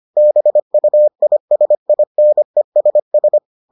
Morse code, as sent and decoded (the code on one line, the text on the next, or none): -... ..- .. ... .. -. ... ...
BUISINESS